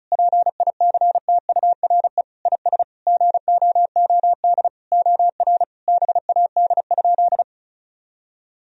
Morse code, heard as a keyed tone, about 36 wpm, 700 Hz. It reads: PICTURE IS GOOD OR BAD?
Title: Sentences from Top 1000 Words - Review Part 1 - 35wpm